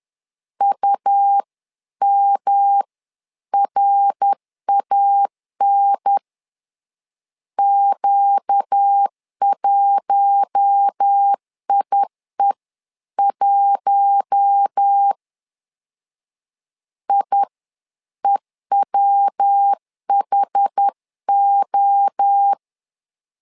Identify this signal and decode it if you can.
Morse code, heard as a keyed tone, about 10 wpm, spaced as for 9 wpm, 800 Hz. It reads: UMRAN Q1IE1 IEWHO
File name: LEZIONI / 4 Q1 10wpm 2448